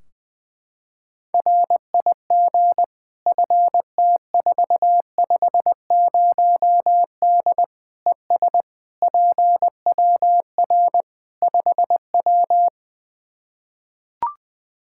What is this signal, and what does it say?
Morse code, heard as a keyed tone, about 20 wpm, 700 Hz. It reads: RIG FT450D ES PWR 5W
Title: Practice QSO - Exchange 2 - Info - 20wpm